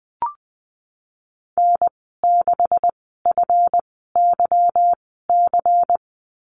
Morse code, heard as a keyed tone, about 20 wpm, spaced as for 15 wpm, 700 Hz. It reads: N6FYC
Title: 154 Lesson 25g - Letter p - Call Signs 20wpm